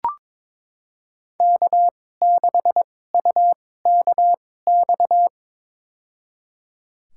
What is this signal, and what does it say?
Morse code, heard as a keyed tone, about 21 wpm, spaced as for 16 wpm, 700 Hz. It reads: K6UKX
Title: USA Call Signs - 5 Characters - 2x Character Spacing - 22wpm